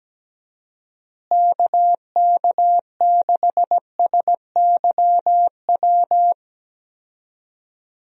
Morse code, heard as a keyed tone, about 17 wpm, 700 Hz. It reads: KK6SYW
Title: USA Call Signs - 6 Characters - Repeated 3x - 17wpm